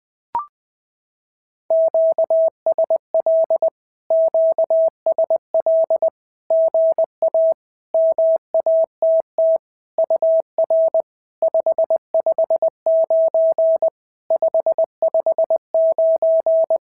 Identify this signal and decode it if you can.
Morse code, heard as a keyed tone, about 21 wpm, 650 Hz. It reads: QSL QSL GA MATT UR 559 559
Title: SOTA QSO Practice 20wpm